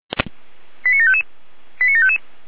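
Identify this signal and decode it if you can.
Five-tone selective call (ZVEI), heard as quick runs of five tones, tones of about 70 ms. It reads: 89755 89755
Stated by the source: ZVEI five-tone selective call received on SDR.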